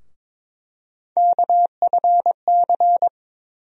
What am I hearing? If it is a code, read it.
Morse code, heard as a keyed tone, about 22 wpm, 700 Hz. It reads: KFC
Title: USA Call Sign Postfixes - 3 Letters 22wpm